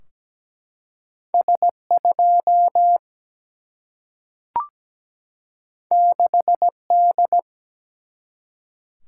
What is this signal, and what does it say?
Morse code, heard as a keyed tone, about 17 wpm, 700 Hz. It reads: S 2 6 D
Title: Single-Letter Single-Number 17wpm